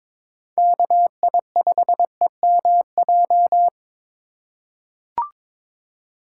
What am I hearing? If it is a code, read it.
Morse code, heard as a keyed tone, about 22 wpm, 700 Hz. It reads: KI5EMJ
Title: USA Call Signs - 4x Character Spacing - 22wpm